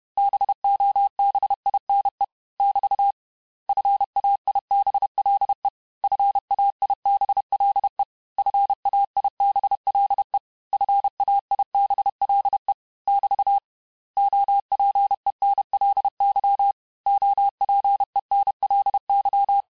Morse code, heard as a keyed tone, about 31 wpm, 800 Hz. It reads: BOBINE <BT> FAIBLE FAIBLE FAIBLE FAIBLE <BT> OPENLY OPENLY
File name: cours-du-mercredi / audios / cours-861-texte-les-mots-vitesse-29MPM